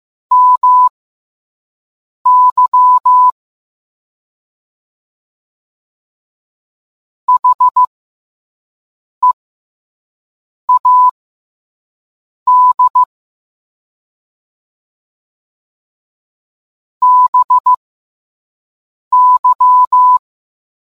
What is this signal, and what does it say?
Morse code, heard as a keyed tone, about 15 wpm, spaced as for 5 wpm, 1000 Hz. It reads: MY HEAD BY